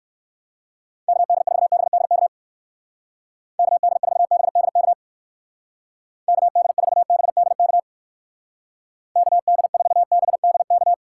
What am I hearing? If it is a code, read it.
Morse code, heard as a keyed tone, about 59 wpm, 700 Hz. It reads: KD4BDK KD4BDK KD4BDK KD4BDK